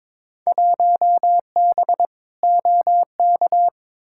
Morse code, heard as a keyed tone, about 22 wpm, 700 Hz. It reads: 1B OK
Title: Field Day Exchanges Encore - 22wpm